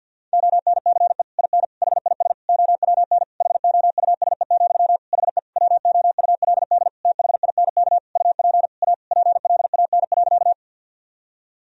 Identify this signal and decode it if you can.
Morse code, heard as a keyed tone, about 52 wpm, 700 Hz. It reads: ONCE IN HIS OWN HOUSE, HE WOULD THINK UP A PLAN.
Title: Sentences from Top 1000 Words - Review Part 2 - 50wpm